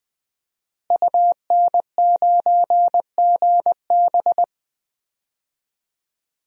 Morse code, heard as a keyed tone, about 20 wpm, 700 Hz. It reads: UN9GB
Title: DXCC-100 Call Signs - 5 Characters - Repeated 2x - 20wpm